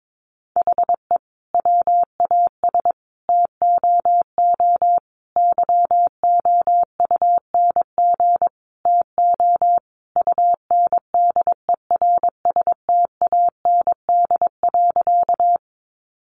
Morse code, heard as a keyed tone, about 22 wpm, 700 Hz. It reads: HE WAS TOO YOUNG TO UNDERSTAND.
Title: Sentences from Top 600 Words 22wpm